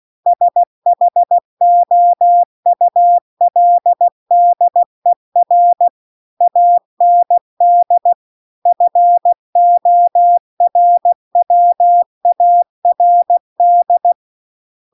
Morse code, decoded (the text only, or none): SHOULDER AND FORWARD